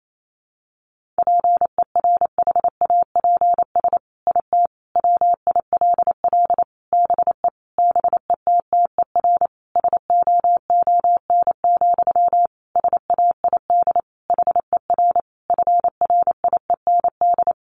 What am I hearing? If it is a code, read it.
Morse code, heard as a keyed tone, about 28 wpm, 700 Hz. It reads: PERHAPS IT WILL BE BETTER SOON, SAID HER FRIEND
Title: Sentences from Top 1500 Words 28wpm